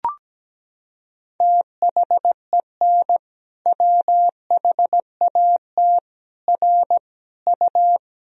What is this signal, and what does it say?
Morse code, heard as a keyed tone, about 17 wpm, 700 Hz. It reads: THEN WHAT R U
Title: Sentences from Top 100 Words and Abbreviations 17wpm